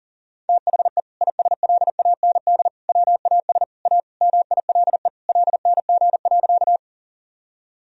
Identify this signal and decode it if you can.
Morse code, heard as a keyed tone, about 37 wpm, 700 Hz. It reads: THE ISLAND WAS A MILE LNG.
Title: Sentences from Top 500 Words and Abbreviations 40wpm